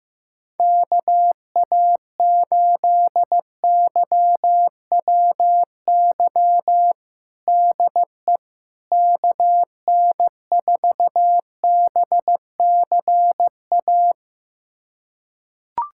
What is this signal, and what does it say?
Morse code, heard as a keyed tone, about 15 wpm, 700 Hz. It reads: KA8YWY DE KN4BCA
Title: USA Callsign DE Callsign - 6 Characters - 15wpm